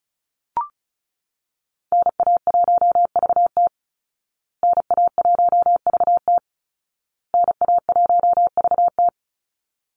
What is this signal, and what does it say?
Morse code, heard as a keyed tone, about 34 wpm, 700 Hz. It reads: NA1VT NA1VT NA1VT
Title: Call Signs Repeated 2x - 35wpm